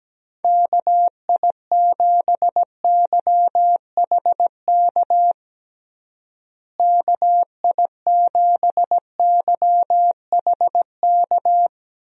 Morse code, heard as a keyed tone, about 17 wpm, 700 Hz. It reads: KI7YHK KI7YHK
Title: USA Call Signs - 6 Characters - Repeated 3x - 17wpm